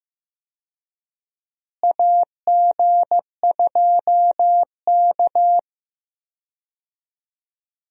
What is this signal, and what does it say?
Morse code, heard as a keyed tone, about 16 wpm, 700 Hz. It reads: AG2K
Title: USA Call Signs - 4 Characters - Repeated 3x - 15wpm